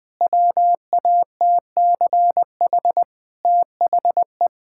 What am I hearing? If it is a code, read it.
Morse code, heard as a keyed tone, about 20 wpm, 700 Hz. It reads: WATCH THE
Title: Sentences from Top 1000 Words - Review Part 5 - 20wpm